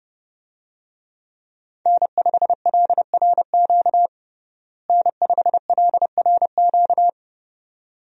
Morse code, heard as a keyed tone, about 31 wpm, 700 Hz. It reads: N5LRQ N5LRQ
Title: USA Call Signs - 5 Characters - Repeated 2x - 30wpm